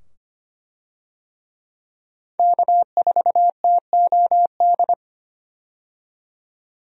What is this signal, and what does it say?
Morse code, heard as a keyed tone, about 25 wpm, 700 Hz. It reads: K4TOD
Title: USA Call Signs - 5 Characters - 25wpm